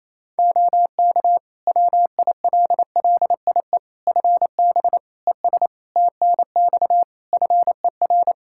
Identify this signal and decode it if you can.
Morse code, heard as a keyed tone, about 27 wpm, 700 Hz. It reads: OK WILLIE FB ES TNX FER
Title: Practice QSO - Exchange 2 - Info - 28wpm